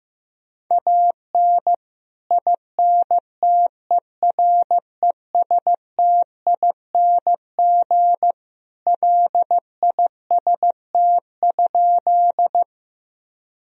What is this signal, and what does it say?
Morse code, heard as a keyed tone, about 15 wpm, 700 Hz. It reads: AN INTERESTING LIST?
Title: Sentences from Top 1000 Words - Review Part 2 - 15wpm